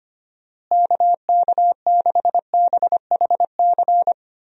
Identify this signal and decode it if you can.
Morse code, heard as a keyed tone, about 25 wpm, 700 Hz. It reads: KK6BHC
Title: USA Call Signs - 6 Characters - Repeated 2x - 25wpm